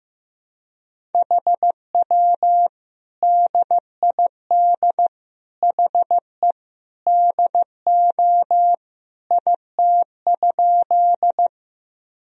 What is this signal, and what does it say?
Morse code, heard as a keyed tone, about 15 wpm, 700 Hz. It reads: HW DID HE DO IT?
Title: Sentences from Top 100 Words and Abbreviations 15wpm